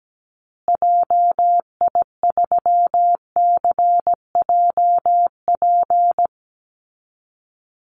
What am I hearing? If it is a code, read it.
Morse code, heard as a keyed tone, about 17 wpm, 700 Hz. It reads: JI3CJP
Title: Call Signs Repeated 2x - 17wpm